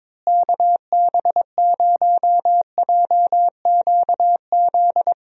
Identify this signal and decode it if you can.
Morse code, heard as a keyed tone, about 21 wpm, 700 Hz. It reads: KB0JQZ
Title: USA Call Signs - 6 Characters - Repeated 3x - 22wpm